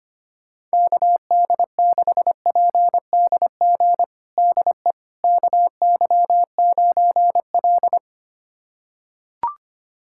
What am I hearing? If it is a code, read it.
Morse code, heard as a keyed tone, about 25 wpm, 700 Hz. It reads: KD6PDG DE KY9L
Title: USA Callsign DE Callsign 25wpm